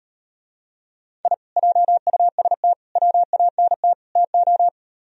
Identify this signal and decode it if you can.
Morse code, heard as a keyed tone, about 38 wpm, 700 Hz. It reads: I JUST WANT TO